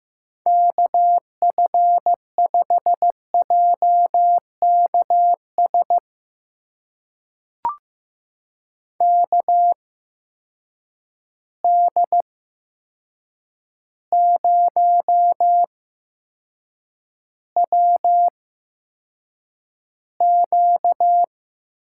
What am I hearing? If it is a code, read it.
Morse code, heard as a keyed tone, about 15 wpm, spaced as for 7 wpm, 700 Hz. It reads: KF5JKS KD0WQ
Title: USA Call Signs - 6 Characters - 8x Character Spacing - 15wpm